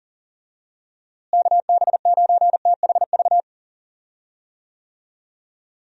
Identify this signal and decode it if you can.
Morse code, heard as a keyed tone, about 39 wpm, 700 Hz. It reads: KB9THV
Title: USA Call Signs - 6 Characters - Repeated 2x - 40wpm